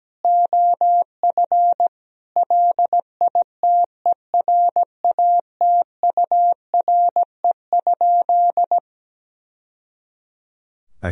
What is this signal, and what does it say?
Morse code, heard as a keyed tone, about 17 wpm, 700 Hz. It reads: OF LITERATURE?
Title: Sentences from Top 1000 Words - Review Part 3 - 17wpm